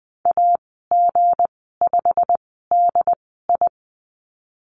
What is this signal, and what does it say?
Morse code, heard as a keyed tone, about 20 wpm, spaced as for 15 wpm, 700 Hz. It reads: AG5DI
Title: 179 Lesson 29g - Slash - Call Signs 20wpm